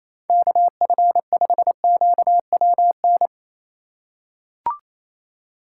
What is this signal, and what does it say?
Morse code, heard as a keyed tone, about 28 wpm, 700 Hz. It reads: KF5QWN
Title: USA Call Signs - 6 Characters - 2x Character Spacing - 28wpm